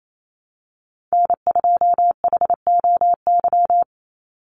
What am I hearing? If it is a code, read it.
Morse code, heard as a keyed tone, about 28 wpm, 700 Hz. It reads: N2HOY
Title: USA Call Signs 28wpm